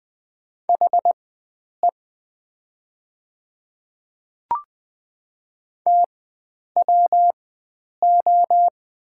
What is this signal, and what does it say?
Morse code, heard as a keyed tone, about 20 wpm, spaced as for 9 wpm, 700 Hz. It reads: HE TWO
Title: 4x Character Spacing - Top 100 Words - 20wpm